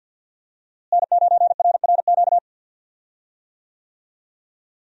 Morse code, heard as a keyed tone, about 52 wpm, 700 Hz. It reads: N9RRQ
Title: USA Call Signs - 5 Characters - Repeated 2x - 50wpm